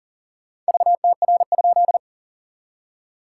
Morse code, heard as a keyed tone, about 40 wpm, 690 Hz. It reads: VTR?
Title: SOTA QSO Practice 40wpm